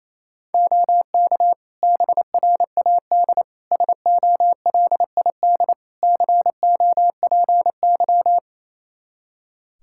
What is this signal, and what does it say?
Morse code, heard as a keyed tone, about 28 wpm, 700 Hz. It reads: OK BRAD SOLID COPY
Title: Practice QSO - Exchange 3 - More Info - 28wpm